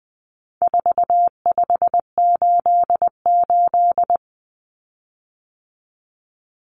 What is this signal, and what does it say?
Morse code, heard as a keyed tone, about 20 wpm, 700 Hz. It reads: 4588
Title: Common Cut Numbers - 4-Digit 20wpm